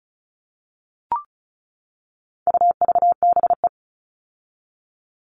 Morse code, heard as a keyed tone, about 34 wpm, 700 Hz. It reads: UVBE
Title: All Cut Numbers - 4-Digit 35wpm